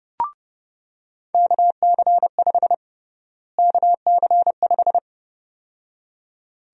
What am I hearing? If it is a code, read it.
Morse code, heard as a keyed tone, about 32 wpm, 700 Hz. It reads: KC5 KC5
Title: USA Call Sign Prefixes - Repeated 2x - 30wpm